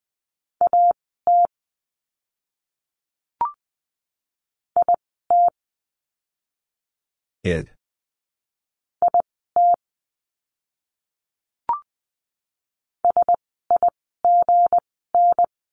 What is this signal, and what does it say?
Morse code, heard as a keyed tone, about 20 wpm, spaced as for 14 wpm, 700 Hz. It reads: AT IT IT SIGN
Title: 160 Lesson 26f - Letter g - Words 20wpm